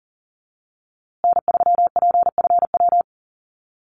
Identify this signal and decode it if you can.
Morse code, heard as a keyed tone, about 40 wpm, 700 Hz. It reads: N3PFW
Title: USA Call Signs Repeated 3x 40wpm